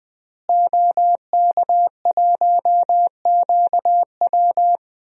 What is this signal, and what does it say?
Morse code, heard as a keyed tone, about 20 wpm, 700 Hz. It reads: OK1QW
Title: DXCC-100 Call Signs - 5 Characters - Repeated 2x - 20wpm